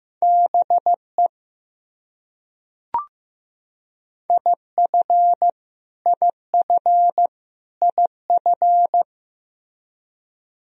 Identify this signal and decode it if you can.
Morse code, heard as a keyed tone, about 15 wpm, 700 Hz. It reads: BE IF IF IF
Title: Top 100 Words Repeated 15wpm